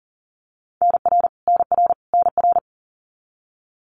Morse code, heard as a keyed tone, about 40 wpm, 700 Hz. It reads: NR NR NR
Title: Abbreviations Repeated 3x - 40wpm